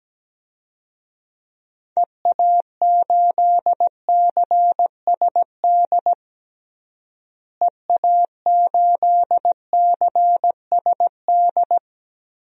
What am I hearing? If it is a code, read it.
Morse code, heard as a keyed tone, about 17 wpm, 700 Hz. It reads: EA8CSD EA8CSD
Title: DXCC-100 Call Signs - 6 Characters - Repeated 3x - 17wpm